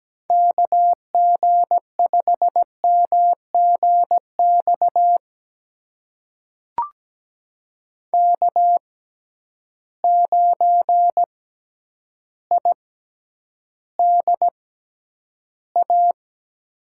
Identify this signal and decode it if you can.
Morse code, heard as a keyed tone, about 17 wpm, spaced as for 9 wpm, 700 Hz. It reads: KG5MGX K9IDA
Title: USA Call Signs - 6x Character Spacing - 17wpm